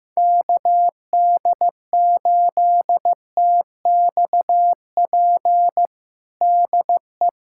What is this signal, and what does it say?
Morse code, heard as a keyed tone, about 16 wpm, 700 Hz. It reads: KD8TXP DE
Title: USA Callsign DE Callsign - 6 Characters - 15wpm